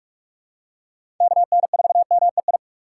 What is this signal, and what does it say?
Morse code, heard as a keyed tone, about 46 wpm, 700 Hz. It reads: KN4MEI
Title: USA Call Signs - 6 Characters - 2x Character Spacing - 45wpm